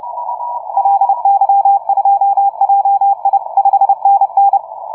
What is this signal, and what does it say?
Morse code, heard as a keyed tone, about 29 wpm, 790 Hz, with radio noise. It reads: LY2JI 5NN
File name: on6jun / OZ ON6JUN-CW-30m